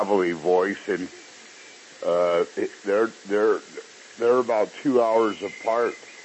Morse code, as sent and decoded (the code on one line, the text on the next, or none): none